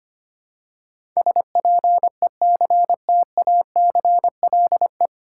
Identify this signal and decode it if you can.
Morse code, heard as a keyed tone, about 25 wpm, 700 Hz. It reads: SPECTACLE
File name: practice / 25wpm / 9 Letter Words - First 3 Letters Given - 25wpm